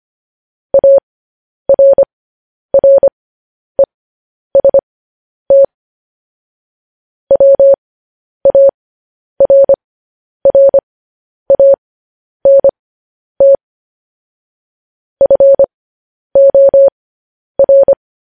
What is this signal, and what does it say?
Morse code, heard as a keyed tone, about 25 wpm, spaced as for 10 wpm, 550 Hz. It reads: ARREST WARRANT FOR
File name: podcast / News headlines-1x-25x10